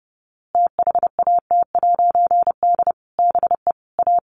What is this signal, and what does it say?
Morse code, heard as a keyed tone, about 30 wpm, 700 Hz. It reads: THAT'D BE A
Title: A Sound Of Thunder 30wpm